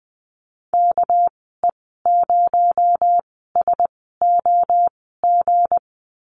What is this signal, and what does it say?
Morse code, heard as a keyed tone, about 20 wpm, spaced as for 14 wpm, 700 Hz. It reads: KE0SOG
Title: 245 Lesson 39g - Letter z - Call Signs 20wpm